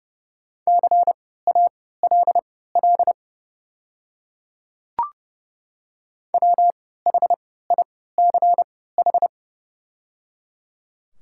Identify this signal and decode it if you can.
Morse code, heard as a keyed tone, about 30 wpm, spaced as for 17 wpm, 700 Hz. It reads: CALL WHICH
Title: 3x Character Spacing - Top 300 Words - 30wpm